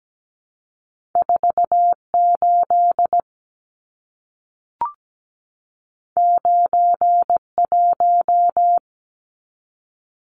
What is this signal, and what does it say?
Morse code, heard as a keyed tone, about 17 wpm, 700 Hz. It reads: 48 91
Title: Numbers - 2-Digit 17wpm